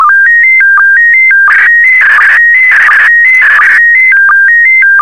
SSTV (slow-scan television), PD 90.
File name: sstv / PD90-VK4MSL